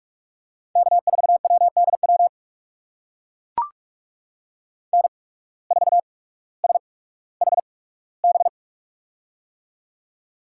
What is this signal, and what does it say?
Morse code, heard as a keyed tone, about 44 wpm, spaced as for 19 wpm, 700 Hz. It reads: K4WDW N4SHB
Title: USA Call Signs - 5 Characters - 8x Character Spacing - 45wpm